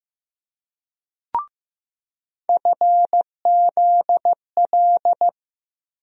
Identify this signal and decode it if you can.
Morse code, heard as a keyed tone, about 15 wpm, 700 Hz. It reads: FZL
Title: USA Call Sign Postfixes - 3 Letters 15wpm